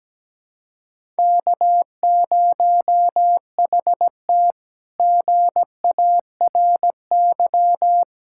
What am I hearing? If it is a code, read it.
Morse code, heard as a keyed tone, about 17 wpm, 700 Hz. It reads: K0HT GARY
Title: K1USN SST QSO Practice - 17wpm